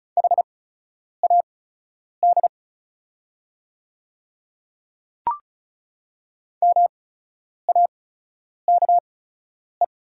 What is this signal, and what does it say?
Morse code, heard as a keyed tone, about 36 wpm, spaced as for 10 wpm, 700 Hz. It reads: HAD MAKE
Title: 8x Character Spacing - Top 100 Words - 35wpm